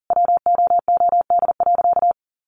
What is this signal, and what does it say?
Morse code, heard as a keyed tone, about 40 wpm, 700 Hz. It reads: WOOD.
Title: Sentences from Top 1300 Words 40wpm